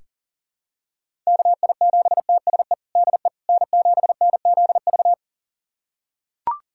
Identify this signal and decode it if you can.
Morse code, heard as a keyed tone, about 40 wpm, 700 Hz. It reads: KI7TSE DE N7NZV
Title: USA Callsign DE Callsign 40wpm